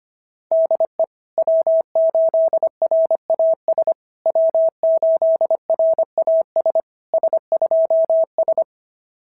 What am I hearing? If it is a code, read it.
Morse code, heard as a keyed tone, about 25 wpm, 690 Hz. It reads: DE W8RAS W8RAS S2S
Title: SOTA QSO Practice 25wpm